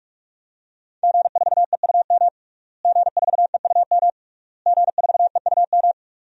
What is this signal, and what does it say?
Morse code, heard as a keyed tone, about 42 wpm, 700 Hz. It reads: G4EUM G4EUM G4EUM
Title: DXCC-100 Call Signs - 5 Characters - Repeated 3x - 45wpm